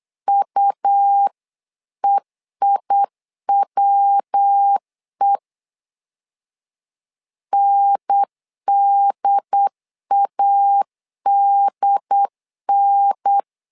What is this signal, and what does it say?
Morse code, heard as a keyed tone, about 8 wpm, 800 Hz. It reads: UEIWE NDADN
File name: LEZIONI / 3 MWRO / 3 MW 8wpm 2501